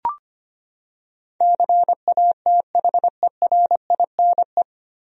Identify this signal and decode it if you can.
Morse code, heard as a keyed tone, about 25 wpm, 700 Hz. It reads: CATHERINE
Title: USA Common Names - 25wpm